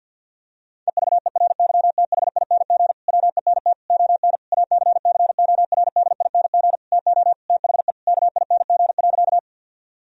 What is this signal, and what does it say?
Morse code, heard as a keyed tone, about 50 wpm, 700 Hz. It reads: EVERYTHING WENT ON ACCORDING TO THE KING.
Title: Sentences from Top 1000 Words - Review Part 4 - 50wpm